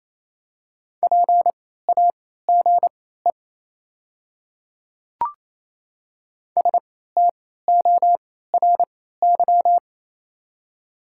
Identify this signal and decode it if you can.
Morse code, heard as a keyed tone, about 28 wpm, spaced as for 16 wpm, 700 Hz. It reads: PAGE STORY
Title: 3x Character Spacing - Top 300 Words - 28wpm